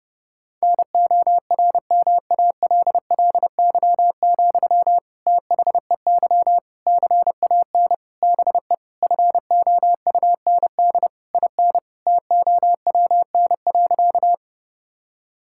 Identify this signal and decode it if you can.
Morse code, heard as a keyed tone, about 32 wpm, 700 Hz. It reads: NORMALLY, THEY CAN BE FOUND IN TOWN.
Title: Sentences from Top 1000 Words - Review Part 6 - 30wpm